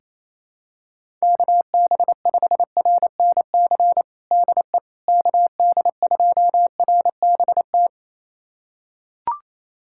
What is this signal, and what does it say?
Morse code, heard as a keyed tone, about 28 wpm, 700 Hz. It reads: KB5RNC DE KD2RBT
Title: USA Callsign DE Callsign - 6 Characters - 28wpm